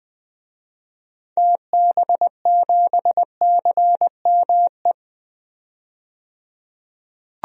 Morse code, heard as a keyed tone, about 21 wpm, 700 Hz. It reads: TB7CME
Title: DXCC-100 Call Signs - 6 Characters - 20wpm